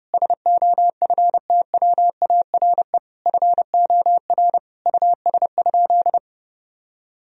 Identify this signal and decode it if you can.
Morse code, heard as a keyed tone, about 30 wpm, 700 Hz. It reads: SOFTWARE FOR US?
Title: Sentences from Top 1000 Words - Review Part 3 - 30wpm